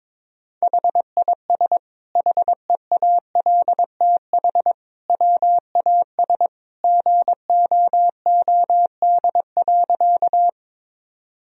Morse code, heard as a keyed tone, about 22 wpm, 700 Hz. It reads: HIS HEALTH WAS GOOD.
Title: Sentences from Top 1000 Words - Review Part 3 - 22wpm